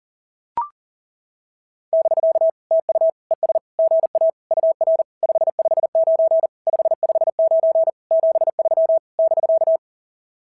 Morse code, heard as a keyed tone, about 40 wpm, 690 Hz. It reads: <BK> TU ES GA UR 559 559 73 <BK>
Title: SOTA QSO Practice 40wpm